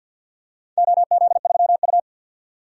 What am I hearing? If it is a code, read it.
Morse code, heard as a keyed tone, about 45 wpm, 700 Hz. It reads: OZ3U
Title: DXCC-100 Call Signs - 4 Characters - 50wpm